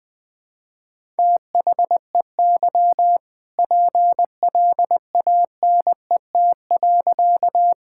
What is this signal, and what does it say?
Morse code, heard as a keyed tone, about 20 wpm, 700 Hz. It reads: THEY PLANET.
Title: All-Age Appropriate Jokes - 20wpm